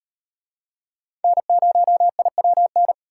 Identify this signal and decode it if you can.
Morse code, heard as a keyed tone, about 36 wpm, 700 Hz. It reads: N0IWN